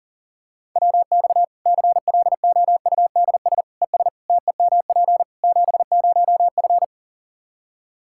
Morse code, heard as a keyed tone, about 40 wpm, 700 Hz. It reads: WX CLOUDS ES TEMP 70F
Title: Practice QSO - Exchange 2 - Info - 40wpm